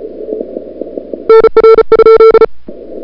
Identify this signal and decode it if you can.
Morse code, heard as a keyed tone, about 36 wpm, 1340 Hz. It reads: NR?
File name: CWSprint / February 2007 CW Sprint / 01-2007 CW Sprint Hour1